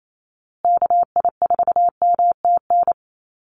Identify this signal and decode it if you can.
Morse code, heard as a keyed tone, about 28 wpm, 700 Hz. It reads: KI4MTN
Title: USA Call Signs Repeated 2x 28wpm